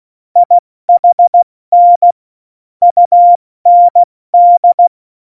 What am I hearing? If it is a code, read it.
Morse code, heard as a keyed tone, about 16 wpm, spaced as for 14 wpm, 700 Hz. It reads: IHN UND
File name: Morsekurs / Witze / Witz 09. Oktober 2018, 70 F80